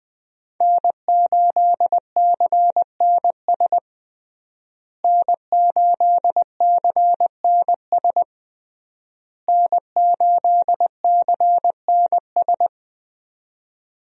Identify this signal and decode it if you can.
Morse code, heard as a keyed tone, about 21 wpm, 700 Hz. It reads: N8CNS N8CNS N8CNS
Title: USA Call Signs - 5 Characters - Repeated 3x - 20wpm